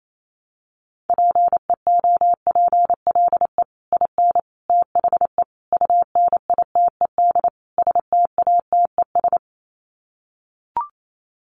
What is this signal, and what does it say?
Morse code, heard as a keyed tone, about 28 wpm, 700 Hz. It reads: PEOPLE IN THE UNITED STATES
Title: Sets of 5 Words Encore 28wpm